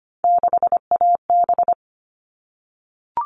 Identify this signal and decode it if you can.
Morse code, heard as a keyed tone, about 25 wpm, 700 Hz. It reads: 6AB
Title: All Cut Numbers - 3-Digit 25wpm